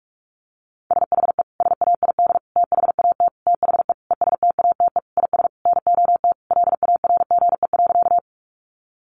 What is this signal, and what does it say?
Morse code, heard as a keyed tone, about 44 wpm, 700 Hz. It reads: SHE SAID THAT THE ESTATE IS NOT LARGE.
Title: Sentences from Top 1200 Words 45wpm